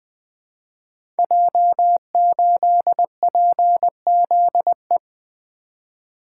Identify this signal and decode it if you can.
Morse code, heard as a keyed tone, about 20 wpm, 700 Hz. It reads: J8PZE
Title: Code Groups - 5 Letter:Numbers - 20wpm